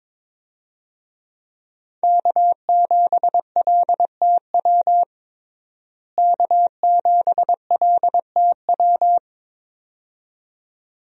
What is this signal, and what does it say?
Morse code, heard as a keyed tone, about 22 wpm, 700 Hz. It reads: K7LTW K7LTW
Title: USA Call Signs - 5 Characters - Repeated 2x - 22wpm